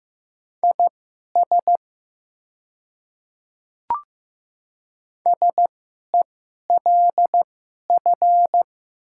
Morse code, heard as a keyed tone, about 15 wpm, spaced as for 11 wpm, 700 Hz. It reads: IS SELF
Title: 2x Character Spacing - Top 300 Words - 15wpm